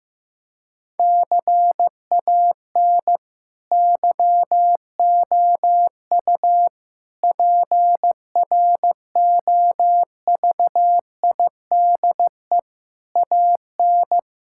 Morse code, heard as a keyed tone, about 15 wpm, 700 Hz. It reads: CAN YOU PROVIDE AN
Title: Sentences from Top 1000 Words - Review Part 5 - 15wpm